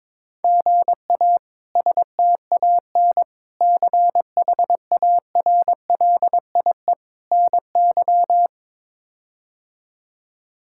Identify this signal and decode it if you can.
Morse code, heard as a keyed tone, about 22 wpm, 700 Hz. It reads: GA STAN CHARLIE NY
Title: K1USN SST QSO Practice - 22wpm